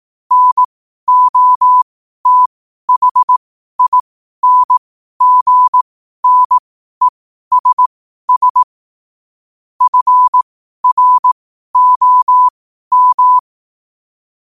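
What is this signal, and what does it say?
Morse code, heard as a keyed tone, about 18 wpm, spaced as for 12 wpm, 1000 Hz. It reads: NOTHINGNESS FROM
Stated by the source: Morse code at 12 wpm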